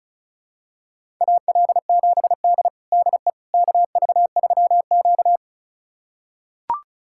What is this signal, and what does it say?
Morse code, heard as a keyed tone, about 34 wpm, 700 Hz. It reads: AL7D DE KV3Q
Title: USA Callsign DE Callsign - 4 Characters - 35wpm